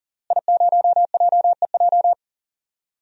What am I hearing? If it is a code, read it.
Morse code, heard as a keyed tone, about 39 wpm, 700 Hz. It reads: I0JEJ
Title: DXCC-100 Call Signs - 5 Characters - Repeated 2x - 40wpm